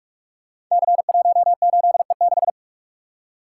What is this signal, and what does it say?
Morse code, heard as a keyed tone, about 44 wpm, 700 Hz. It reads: C18E6
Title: Code Groups - 5 Letter:Numbers - 45wpm